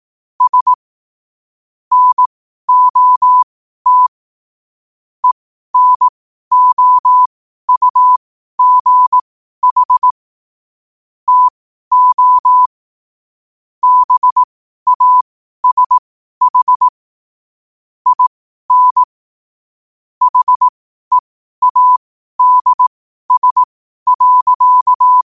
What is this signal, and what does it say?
Morse code, heard as a keyed tone, about 18 wpm, spaced as for 12 wpm, 1000 Hz. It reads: S NOT ENOUGH TO BASH IN HEADS.
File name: 2026-01-07-12wpm